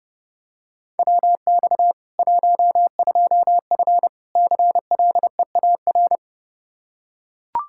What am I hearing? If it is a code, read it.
Morse code, heard as a keyed tone, about 30 wpm, 700 Hz. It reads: WX 12F CLEAR
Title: Practice QSO - Exchange 2 - Info - 30wpm